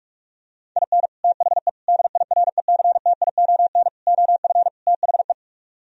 Morse code, heard as a keyed tone, about 45 wpm, 700 Hz. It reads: IN THE DIRECTION OF THE